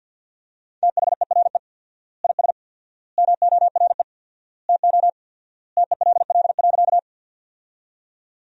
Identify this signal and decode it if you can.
Morse code, heard as a keyed tone, about 50 wpm, 700 Hz. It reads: THERE IS MORE TO TELL.
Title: 4x Word-Spacing - Sentences from Top 200 Words - 50wpm